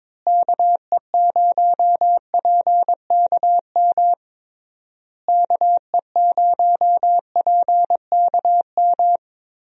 Morse code, heard as a keyed tone, about 22 wpm, 700 Hz. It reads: KE0PKM KE0PKM
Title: USA Call Signs - 6 Characters - Repeated 3x - 22wpm